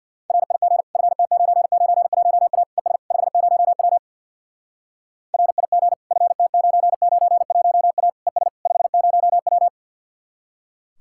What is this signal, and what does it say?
Morse code, heard as a keyed tone, about 64 wpm, 700 Hz. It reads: RIG FT991A ES 50W RIG FT991A ES 50W